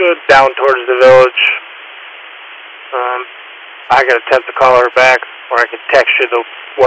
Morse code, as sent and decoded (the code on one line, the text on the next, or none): none